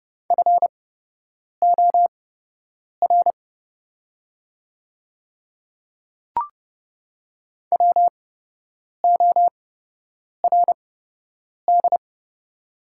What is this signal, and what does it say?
Morse code, heard as a keyed tone, about 30 wpm, spaced as for 8 wpm, 700 Hz. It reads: FOR WORD
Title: 8x Character Spacing - Top 100 Words - 30wpm